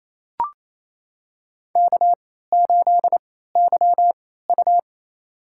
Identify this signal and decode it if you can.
Morse code, heard as a keyed tone, about 28 wpm, spaced as for 16 wpm, 700 Hz. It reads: K8YU
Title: USA Call Signs - 4 Characters - 3x Character Spacing - 28wpm